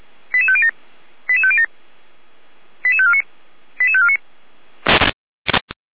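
Five-tone selective call (ZVEI), heard as quick runs of five tones, tones of about 70 ms. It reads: 88507 88507 88549 88549
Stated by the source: ZVEI five-tone selective call received on SDR.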